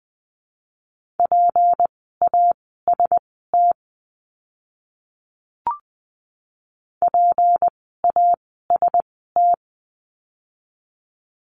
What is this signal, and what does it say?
Morse code, heard as a keyed tone, about 20 wpm, spaced as for 14 wpm, 700 Hz. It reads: PAST PAST
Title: 153 Lesson 25f - Letter p - Words 20wpm